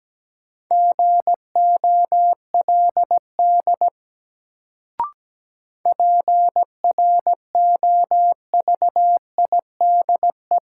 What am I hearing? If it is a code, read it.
Morse code, heard as a keyed tone, about 17 wpm, 700 Hz. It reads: GOLD PROVIDE
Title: Sentences from Top 1100 Words - New Word Drill Alt - 17wpm